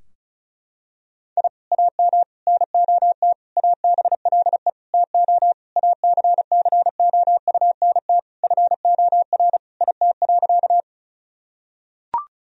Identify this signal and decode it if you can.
Morse code, heard as a keyed tone, about 34 wpm, 700 Hz. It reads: I AM NOT ABLE TO ACCOUNT FOR IT.
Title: Sentences from Top 1000 Words - Review Part 2 - 35wpm